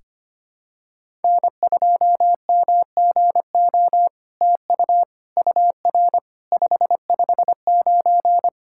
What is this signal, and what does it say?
Morse code, heard as a keyed tone, about 25 wpm, 700 Hz. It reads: N2MGO TU UR 559
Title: SOTA QSO Practice 25wpm